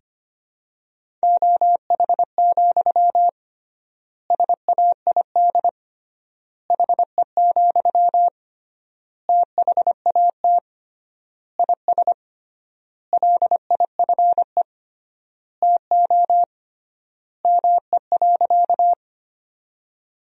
Morse code, heard as a keyed tone, about 25 wpm, 700 Hz. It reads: OH, SAID HE, THAT IS LIFE TO ME.
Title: 3x Word-Spacing - Sentences from Top 500 Words - 25wpm